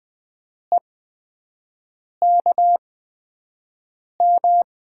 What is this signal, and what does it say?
Morse code, heard as a keyed tone, about 20 wpm, spaced as for 5 wpm, 700 Hz. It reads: EKM
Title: USA Call Signs - 6 Characters - 8x Character Spacing - 20wpm